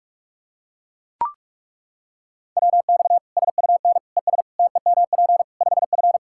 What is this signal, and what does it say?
Morse code, heard as a keyed tone, about 41 wpm, 700 Hz. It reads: WX SUN ES TEMP 5F
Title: Practice QSO - Exchange 2 - Info - 45wpm